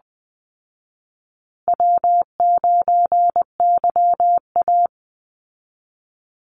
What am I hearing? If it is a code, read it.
Morse code, heard as a keyed tone, about 20 wpm, 700 Hz. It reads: W9YA
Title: Call Signs Repeated 2x - 20wpm